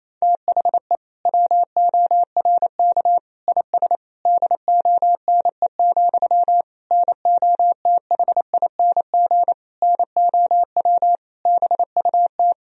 Morse code, heard as a keyed tone, about 28 wpm, 700 Hz. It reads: THE WORK IS DONE, NOTHING NOW BUT
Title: Sentences from Top 1000 Words - Review Part 4 - 28wpm